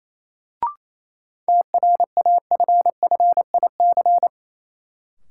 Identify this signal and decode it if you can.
Morse code, heard as a keyed tone, about 27 wpm, 700 Hz. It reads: TRAFFIC
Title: Sentences from Top 1300 Words - New Word Drill Alt - 28wpm